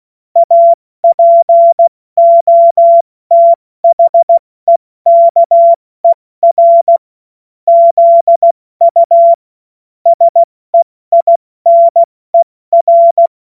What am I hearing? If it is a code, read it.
Morse code, heard as a keyed tone, about 16 wpm, spaced as for 14 wpm, 680 Hz. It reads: APOTHEKER ZU SEINER
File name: Morsekurs / Witze / Witz 25. Mai 2021, 70 F80